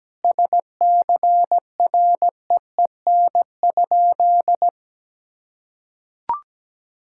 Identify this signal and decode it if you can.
Morse code, heard as a keyed tone, about 17 wpm, 700 Hz. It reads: SCREEN?
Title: Sentences from Top 1000 Words - Review Part 6 - 17wpm